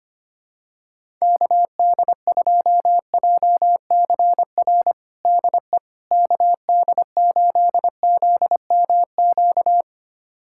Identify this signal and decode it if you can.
Morse code, heard as a keyed tone, about 25 wpm, 700 Hz. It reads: KD2JCR DE KD8ZMQ
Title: USA Callsign DE Callsign 25wpm